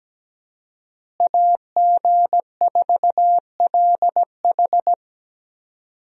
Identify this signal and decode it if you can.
Morse code, heard as a keyed tone, about 17 wpm, 700 Hz. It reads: AG4LH
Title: USA Call Signs - 5 Characters - Repeated 3x - 17wpm